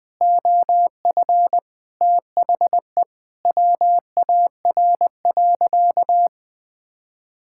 Morse code, heard as a keyed tone, about 21 wpm, 700 Hz. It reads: OF THE WAR.
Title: Sentences with 4-Word Phrases - 20wpm